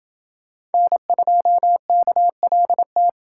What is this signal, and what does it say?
Morse code, heard as a keyed tone, about 28 wpm, 700 Hz. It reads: N2KLT